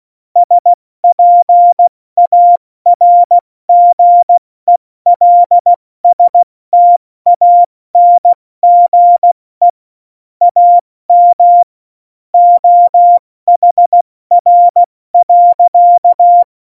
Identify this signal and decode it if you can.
Morse code, heard as a keyed tone, about 16 wpm, 700 Hz. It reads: SPARGELSTANGE AM OHR.
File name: Morsekurs / Witze / Witz 16. April 2019, 70 F80